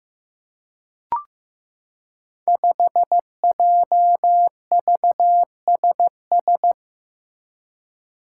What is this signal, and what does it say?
Morse code, heard as a keyed tone, about 15 wpm, 700 Hz. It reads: 5JVSS
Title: Five Challenging Letter-Number Combinations 15wpm